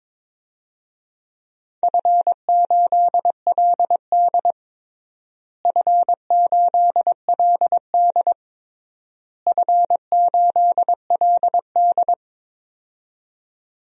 Morse code, heard as a keyed tone, about 22 wpm, 700 Hz. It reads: F8LD F8LD F8LD
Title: DXCC-100 Call Signs - 4 Characters - Repeated 3x - 22wpm